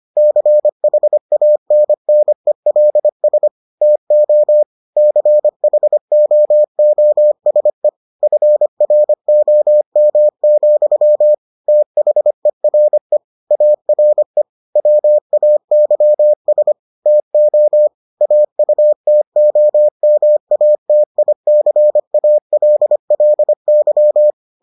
Morse code, heard as a keyed tone, about 25 wpm, 600 Hz. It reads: CHANNELS TO CHOOSE FROM, THERE ARE WAYS TO AUTOMATICALLY